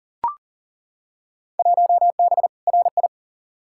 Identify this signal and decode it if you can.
Morse code, heard as a keyed tone, about 39 wpm, 700 Hz. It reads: 1B RI
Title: Field Day Exchanges Encore - 40wpm